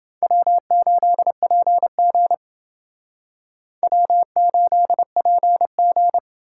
Morse code, heard as a keyed tone, about 28 wpm, 700 Hz. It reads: W8PG W8PG